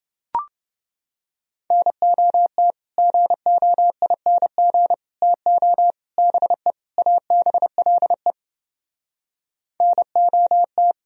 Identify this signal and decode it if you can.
Morse code, heard as a keyed tone, about 30 wpm, 700 Hz. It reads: NOT GOING TO BE ABLE NOT